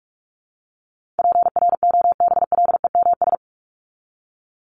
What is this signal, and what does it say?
Morse code, heard as a keyed tone, about 44 wpm, 700 Hz. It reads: PROBLEMS
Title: ES:S Suffix - Top 100 Words with Base Words 45wpm